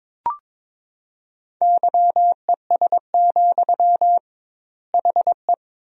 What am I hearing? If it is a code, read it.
Morse code, heard as a keyed tone, about 22 wpm, 700 Hz. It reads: YES, HE
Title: 2x Word-Spacing - Sentences from Top 500 Words - 22wpm